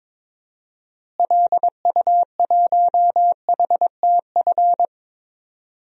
Morse code, heard as a keyed tone, about 22 wpm, 700 Hz. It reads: LU1HTF
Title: DXCC-100 Call Signs - 6 Characters - Repeated 2x - 22wpm